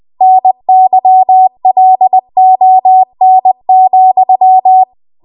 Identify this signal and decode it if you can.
Morse code, heard as a keyed tone, about 20 wpm, 750 Hz. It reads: NYLON,
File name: Archive / 20 WPM / 20WPM